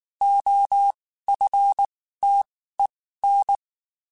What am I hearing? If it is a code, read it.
Morse code, heard as a keyed tone, about 19 wpm, spaced as for 14 wpm, 800 Hz. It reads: OFTEN
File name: (spaced) / QSOs Spaced-mp3s / kc7haa @18 sp